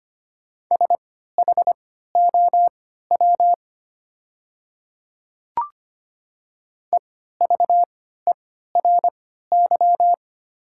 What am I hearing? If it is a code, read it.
Morse code, heard as a keyed tone, about 25 wpm, spaced as for 14 wpm, 690 Hz. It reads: SHOW EVERY
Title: 3x Character Spacing - Top 300 Words - 25wpm